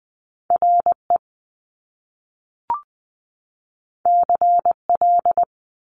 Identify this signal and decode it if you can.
Morse code, heard as a keyed tone, about 20 wpm, 700 Hz. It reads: RE CL
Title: Top Two Letter Combinations in Words 20wpm